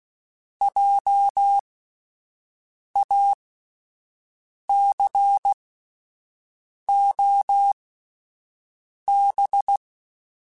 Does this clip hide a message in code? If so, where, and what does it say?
Yes: Morse code from 0.6 to 9.8 s, heard as a keyed tone, about 16 wpm, spaced as for 5 wpm, 800 Hz. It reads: JACOB